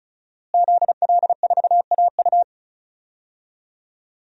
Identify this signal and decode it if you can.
Morse code, heard as a keyed tone, about 36 wpm, 700 Hz. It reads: ZL4AU
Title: DXCC-100 Call Signs - 5 Characters - Repeated 2x - 35wpm